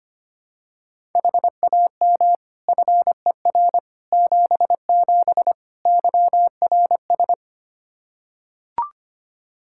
Morse code, heard as a keyed tone, about 25 wpm, 700 Hz. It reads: HAM FER 77 YRS
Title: K1USN SST QSO Practice 25wpm